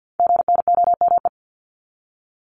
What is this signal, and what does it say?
Morse code, heard as a keyed tone, about 50 wpm, 700 Hz. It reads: GNOME
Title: Multi-letter Phonemes 50wpm